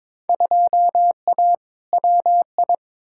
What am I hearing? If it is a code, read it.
Morse code, heard as a keyed tone, about 22 wpm, 700 Hz. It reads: 2A WI
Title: Field Day Exchanges - 22wpm